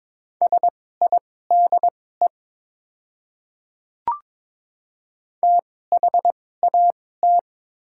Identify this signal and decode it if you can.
Morse code, heard as a keyed tone, about 22 wpm, spaced as for 16 wpm, 690 Hz. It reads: SIDE THAT
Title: 2x Character Spacing - Top 200 Words - 22wpm